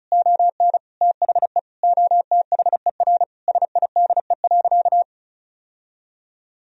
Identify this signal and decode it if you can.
Morse code, heard as a keyed tone, about 36 wpm, 700 Hz. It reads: ON THE OTHER SIDE.
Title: Sentences from Top 1000 Words - Review Part 1 - 35wpm